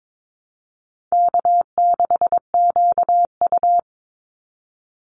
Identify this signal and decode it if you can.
Morse code, heard as a keyed tone, about 22 wpm, 700 Hz. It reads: K6QU
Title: Easy Call Signs - 22wpm